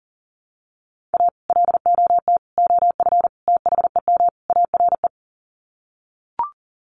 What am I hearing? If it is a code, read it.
Morse code, heard as a keyed tone, about 40 wpm, 700 Hz. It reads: A LOT OF THEM ARE
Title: Sets of 5 Words Encore2 40wpm